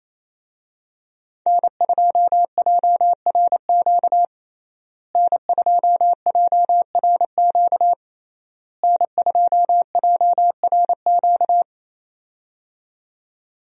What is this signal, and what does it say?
Morse code, heard as a keyed tone, about 28 wpm, 700 Hz. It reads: N2JRQ N2JRQ N2JRQ
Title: USA Call Signs - 5 Characters - Repeated 3x - 28wpm